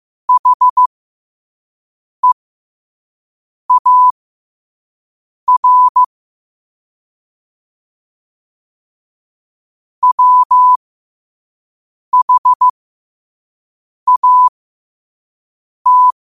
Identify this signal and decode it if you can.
Morse code, heard as a keyed tone, about 15 wpm, spaced as for 5 wpm, 1000 Hz. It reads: HEAR WHAT
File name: qotdfiles / 5wpm